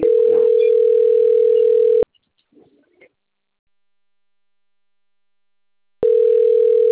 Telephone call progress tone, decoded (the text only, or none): ringback tone